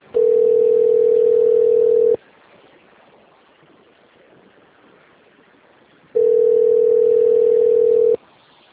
Ringback tone (North America), 440 + 480 Hz.